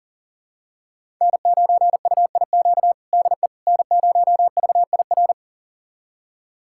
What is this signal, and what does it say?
Morse code, heard as a keyed tone, about 40 wpm, 700 Hz. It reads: N9UIQ DE N0VIR
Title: USA Callsign DE Callsign - 5 Characters - 40wpm